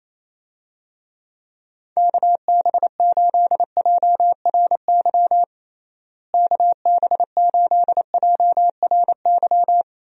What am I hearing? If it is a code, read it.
Morse code, heard as a keyed tone, about 28 wpm, 700 Hz. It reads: KB8JRY KB8JRY
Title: USA Call Signs - 6 Characters - Repeated 2x - 28wpm